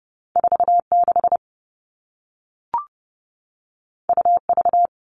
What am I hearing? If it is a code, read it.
Morse code, heard as a keyed tone, about 30 wpm, 700 Hz. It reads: 46 UV
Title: All Cut Numbers - 2-Digit 30wpm